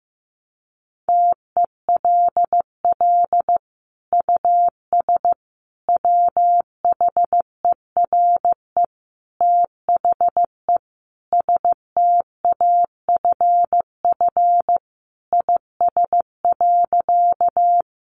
Morse code, heard as a keyed tone, about 15 wpm, 700 Hz. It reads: TELL US WHERE THE STAFF IS.
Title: Sentences from Top 1200 Words 15wpm